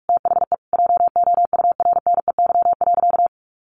Morse code, heard as a keyed tone, about 44 wpm, 700 Hz. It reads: THE JOURNEY.
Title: A Sound Of Thunder 45wpm